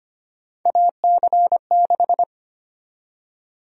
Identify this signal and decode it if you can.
Morse code, heard as a keyed tone, about 25 wpm, 700 Hz. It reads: AC6
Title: USA Call Sign Prefixes - 25wpm